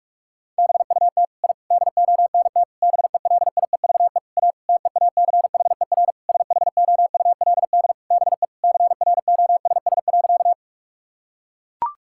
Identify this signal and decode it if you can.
Morse code, heard as a keyed tone, about 42 wpm, 700 Hz. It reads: BUT I DONT BELIEVE A TEACHER SHOULD BE CROSS.
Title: Sentences from Top 1000 Words - Review Part 6 - 45wpm